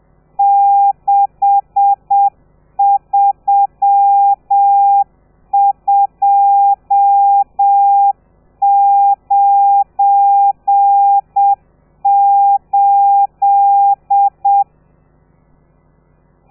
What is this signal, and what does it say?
Morse code, heard as a keyed tone, about 7 wpm, 800 Hz. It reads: 63298
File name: telegrafia / i2tfj / Lez12 30